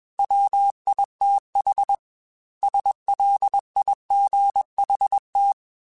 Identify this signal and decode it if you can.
Morse code, heard as a keyed tone, about 21 wpm, 800 Hz. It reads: WITH SLIGHT